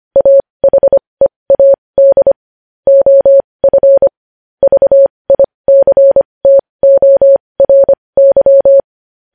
Morse code, heard as a keyed tone, about 25 wpm, spaced as for 20 wpm, 550 Hz. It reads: AHEAD OF VICTORY
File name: podcast / News headlines-1x-25x20